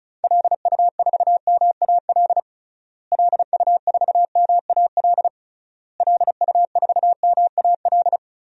Morse code, heard as a keyed tone, about 36 wpm, 700 Hz. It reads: LU4MAL LU4MAL LU4MAL
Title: DXCC-100 Call Signs - 6 Characters - Repeated 3x - 35wpm